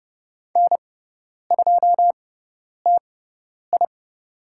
Morse code, heard as a keyed tone, about 30 wpm, spaced as for 10 wpm, 700 Hz. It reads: N2TI
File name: Lesson 18g - Letter m - Call Signs 30wpm